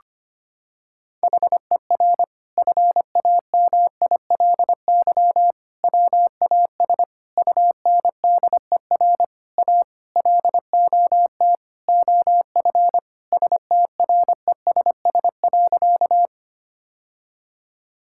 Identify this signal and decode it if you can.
Morse code, heard as a keyed tone, about 25 wpm, 700 Hz. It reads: HER FAMILY WAS UNDER A LOT OF STRESS.
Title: Sentences from Top 1000 Words - Review Part 6 - 25wpm